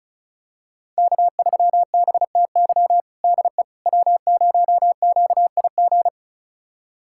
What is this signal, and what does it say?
Morse code, heard as a keyed tone, about 34 wpm, 700 Hz. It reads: K3BTY DE W0QIG
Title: USA Callsign DE Callsign - 5 Characters - 35wpm